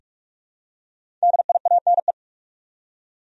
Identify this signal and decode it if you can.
Morse code, heard as a keyed tone, about 44 wpm, 700 Hz. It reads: DIANE
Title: USA Common Names - Max 5 Letters - 45wpm